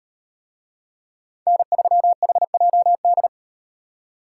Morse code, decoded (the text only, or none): N3HJD